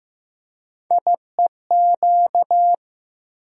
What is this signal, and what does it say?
Morse code, heard as a keyed tone, about 15 wpm, 700 Hz. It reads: IEQ
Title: USA Call Sign Postfixes - 3 Letters 15wpm